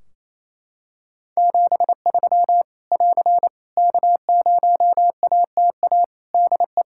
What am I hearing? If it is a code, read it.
Morse code, heard as a keyed tone, about 28 wpm, 700 Hz. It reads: 73 <AR> K0ATA DE
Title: Practice QSO - Exchange 4 - Ending - 28wpm